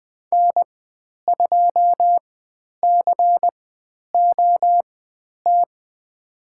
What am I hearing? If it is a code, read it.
Morse code, heard as a keyed tone, about 20 wpm, spaced as for 10 wpm, 700 Hz. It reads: N2COT
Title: 109 Lesson 18g - Letter m - Call Signs 20wpm